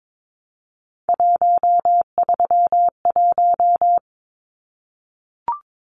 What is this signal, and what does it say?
Morse code, heard as a keyed tone, about 22 wpm, 700 Hz. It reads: 131
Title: Numbers - 3-Digit 22wpm